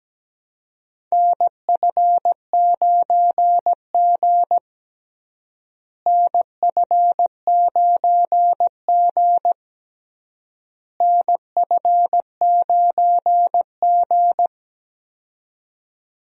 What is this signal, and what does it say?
Morse code, heard as a keyed tone, about 17 wpm, 700 Hz. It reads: NF9G NF9G NF9G
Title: USA Call Signs - 4 Characters - Repeated 3x - 17wpm